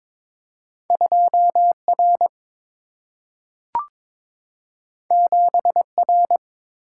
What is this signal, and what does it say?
Morse code, heard as a keyed tone, about 22 wpm, 700 Hz. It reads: 2 R 7 R
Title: Single-Letter Single-Number 22wpm